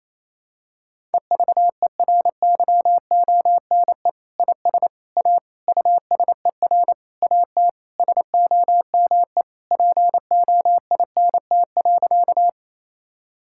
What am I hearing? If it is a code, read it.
Morse code, heard as a keyed tone, about 28 wpm, 700 Hz. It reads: EVERYONE IS A USER AT SOME POINT.
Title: Sentences from Top 1000 Words - Review Part 5 - 28wpm